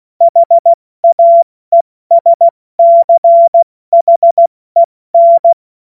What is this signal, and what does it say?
Morse code, heard as a keyed tone, about 16 wpm, 680 Hz. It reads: HAESCHEN
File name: Morsekurs / Witze / Witz 25. Mai 2021, 70 F80